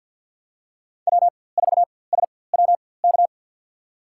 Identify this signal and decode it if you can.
Morse code, heard as a keyed tone, about 47 wpm, spaced as for 23 wpm, 700 Hz. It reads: W4SWK
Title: USA Call Signs - 5 Characters - 4x Character Spacing - 50wpm